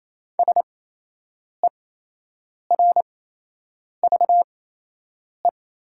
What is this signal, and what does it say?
Morse code, heard as a keyed tone, about 28 wpm, spaced as for 8 wpm, 700 Hz. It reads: SERVE
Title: 8x Character Spacing - Top 500 Words - 28wpm